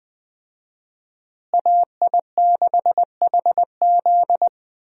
Morse code, heard as a keyed tone, about 20 wpm, 700 Hz. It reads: AI6HZ
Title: USA Call Signs - 5 Characters - Repeated 3x - 20wpm